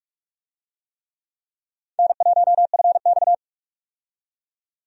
Morse code, heard as a keyed tone, about 46 wpm, 700 Hz. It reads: N1FX
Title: USA Call Signs - 4 Characters - 45wpm